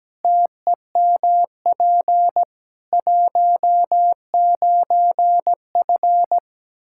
Morse code, heard as a keyed tone, about 17 wpm, 700 Hz. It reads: TEMP 19F
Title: Practice QSO - Exchange 2 - Info - 17wpm